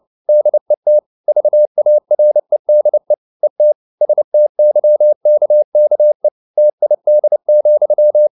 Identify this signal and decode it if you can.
Morse code, heard as a keyed tone, about 29 wpm, 600 Hz. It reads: DET VAREDE ET STYKKE TID,
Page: CW med Gnister